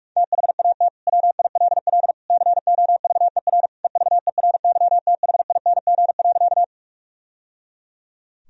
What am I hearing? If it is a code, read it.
Morse code, heard as a keyed tone, about 46 wpm, 700 Hz. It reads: THAT WILL COVER EVERYTHING.
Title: Sentences from Top 1000 Words - Review Part 1 - 45wpm